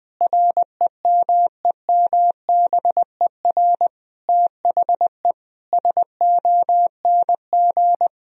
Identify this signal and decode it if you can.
Morse code, heard as a keyed tone, about 20 wpm, 700 Hz. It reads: REMEMBER THE SONG
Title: Sentences from Top 1000 Words - Review Part 2 - 20wpm